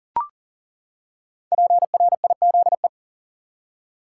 Morse code, heard as a keyed tone, about 40 wpm, 700 Hz. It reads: PRIZE
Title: Five Letter Words - 40wpm